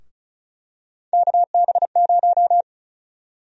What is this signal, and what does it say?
Morse code, heard as a keyed tone, about 34 wpm, 700 Hz. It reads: KB0
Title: USA Call Sign Prefixes - 35wpm